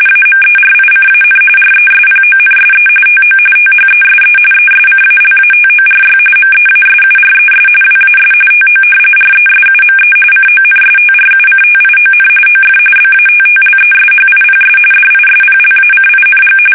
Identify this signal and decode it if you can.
RTTY, about 45.45 baud. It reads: AMERICAN RESPONSE TO THESE ESCALATORY MEASURES WAS TO RESUME CONSTRUCTION OF BATTLESHIPS DISPLACING
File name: fsk850 / battleshipnj